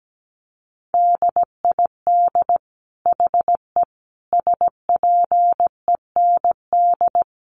Morse code, heard as a keyed tone, about 17 wpm, 700 Hz. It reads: DID HE SPEND
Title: Sentences from Top 800 Words 17wpm